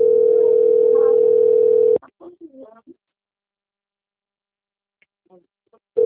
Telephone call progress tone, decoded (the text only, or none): ringback tone